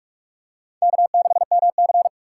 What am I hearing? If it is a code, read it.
Morse code, heard as a keyed tone, about 40 wpm, 700 Hz. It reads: K6MC